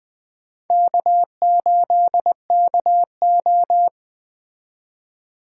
Morse code, heard as a keyed tone, about 20 wpm, 700 Hz. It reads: K8KO
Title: Speed Racing - USA 1x2 Call Signs 17wpm